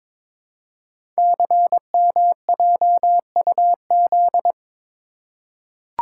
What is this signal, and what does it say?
Morse code, heard as a keyed tone, about 22 wpm, 700 Hz. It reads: CMJUZ
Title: Code Groups - 5 Letters - 22wpm